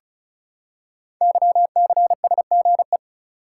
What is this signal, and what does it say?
Morse code, heard as a keyed tone, about 35 wpm, 700 Hz. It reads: YCSGE
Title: Code Groups - 5 Letters - 35wpm